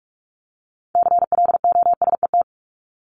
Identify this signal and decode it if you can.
Morse code, heard as a keyed tone, about 44 wpm, 700 Hz. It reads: CLOSET
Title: Top 1751-2000 Words 45wpm